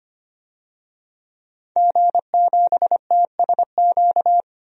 Morse code, heard as a keyed tone, about 25 wpm, 700 Hz. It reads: G7TSQ
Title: DXCC-100 Call Signs - 5 Characters - 25wpm